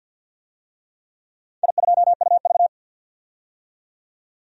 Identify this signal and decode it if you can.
Morse code, heard as a keyed tone, about 52 wpm, 700 Hz. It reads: I2UV
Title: DXCC-100 Call Signs - 4 Characters - 50wpm